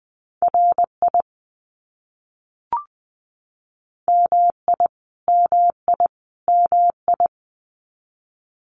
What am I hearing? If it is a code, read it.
Morse code, heard as a keyed tone, about 20 wpm, 700 Hz. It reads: RI MI MI MI
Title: US States Repeated 3x 20wpm